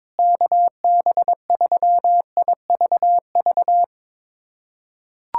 Morse code, heard as a keyed tone, about 22 wpm, 700 Hz. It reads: KB3IVV
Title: USA Callsign DE Callsign - 6 Characters - 22wpm